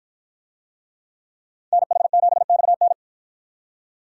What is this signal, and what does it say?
Morse code, heard as a keyed tone, about 53 wpm, 700 Hz. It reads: NH7XN